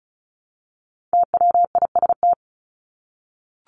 Morse code, heard as a keyed tone, about 34 wpm, 700 Hz. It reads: TWIST
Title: Top 1751-2000 Words 35wpm